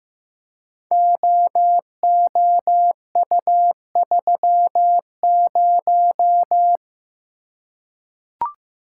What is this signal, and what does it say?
Morse code, heard as a keyed tone, about 15 wpm, 700 Hz. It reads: OOU30
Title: Code Groups - 5 Letter:Numbers - 15wpm